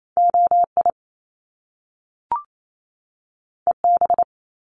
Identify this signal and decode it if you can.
Morse code, heard as a keyed tone, about 28 wpm, 700 Hz. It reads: OI EB
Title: Top Two Letter Combinations in Words 28wpm